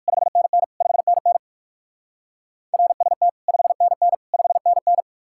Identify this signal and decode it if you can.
Morse code, heard as a keyed tone, about 58 wpm, 700 Hz. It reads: 5NN 5NN RST 5NN 5NN